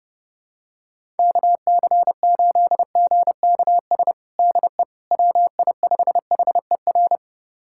Morse code, heard as a keyed tone, about 30 wpm, 700 Hz. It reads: KC8GKS DE WI5HER
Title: USA Callsign DE Callsign 30wpm